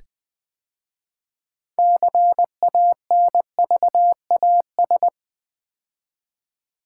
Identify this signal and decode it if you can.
Morse code, heard as a keyed tone, about 20 wpm, 700 Hz. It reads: CANVAS
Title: 6 Letter Words - First 3 Letters Given - 20wpm